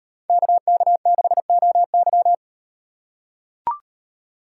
Morse code, heard as a keyed tone, about 38 wpm, 700 Hz. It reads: KK6OY